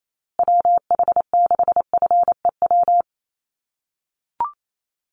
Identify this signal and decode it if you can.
Morse code, heard as a keyed tone, about 28 wpm, 700 Hz. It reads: WH6FEW
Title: USA Call Signs Repeated 3x 28wpm